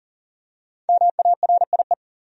Morse code, heard as a keyed tone, about 40 wpm, 700 Hz. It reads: MARIE
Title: USA Common Names - Max 5 Letters - 40wpm